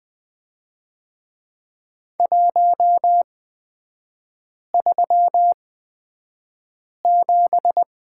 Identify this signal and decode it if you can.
Morse code, heard as a keyed tone, about 20 wpm, spaced as for 5 wpm, 690 Hz. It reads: 137
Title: 229 Lesson 37e - Letter x - Three Characters 20wpm